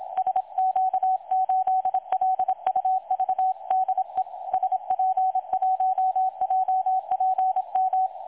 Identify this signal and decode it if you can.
Morse code, heard as a keyed tone, about 25 wpm, 750 Hz. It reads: SQ8LUV DE SP1JPM
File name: dlazaawansowanych / nagraniazpasma / websdr recording 3565.0kHz